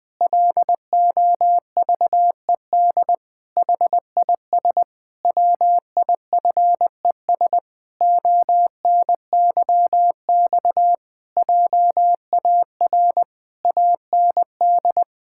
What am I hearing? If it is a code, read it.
Morse code, heard as a keyed tone, about 20 wpm, 700 Hz. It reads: LOVED HIS WIFES ONYX JAR AND
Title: Pangram Challenge v2 - 20wpm